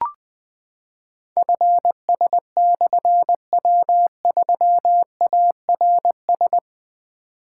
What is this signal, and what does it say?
Morse code, heard as a keyed tone, about 20 wpm, 700 Hz. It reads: FS/W3ARS
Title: Difficult Call Signs 20wpm